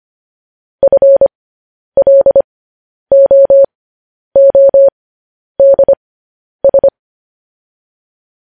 Morse code, heard as a keyed tone, about 25 wpm, spaced as for 10 wpm, 550 Hz. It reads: FLOODS